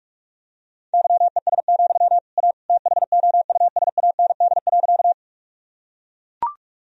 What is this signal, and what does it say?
Morse code, heard as a keyed tone, about 42 wpm, 700 Hz. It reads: YES, A THOUSAND.
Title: Sentences from Top 1000 Words - Review Part 3 - 45wpm